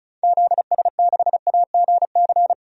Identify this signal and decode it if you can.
Morse code, heard as a keyed tone, about 34 wpm, 700 Hz. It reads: ZS6AGC
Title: DXCC-100 Call Signs - 6 Characters - 35wpm